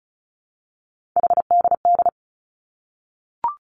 Morse code, heard as a keyed tone, about 36 wpm, 700 Hz. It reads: HDD
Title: Three Challenging Letter-Number Combinations 35wpm